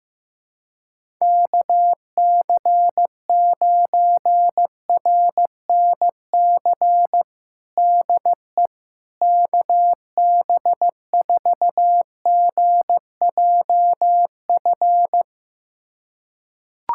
Morse code, heard as a keyed tone, about 15 wpm, 700 Hz. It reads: KC9RNC DE KB4GJF
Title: USA Callsign DE Callsign 15wpm